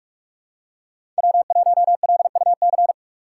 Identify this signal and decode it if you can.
Morse code, heard as a keyed tone, about 42 wpm, 700 Hz. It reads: W1LUC
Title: USA Call Signs - 4x Character Spacing - 45wpm